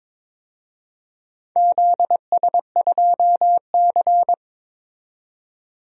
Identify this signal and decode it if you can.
Morse code, heard as a keyed tone, about 22 wpm, 700 Hz. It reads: ZS2C
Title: DXCC-100 Call Signs - 4 Characters - 22wpm